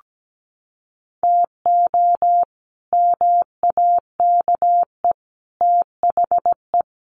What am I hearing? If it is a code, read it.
Morse code, heard as a keyed tone, about 17 wpm, 700 Hz. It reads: TO MAKE THE
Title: Sets of 5 Words Encore2 17wpm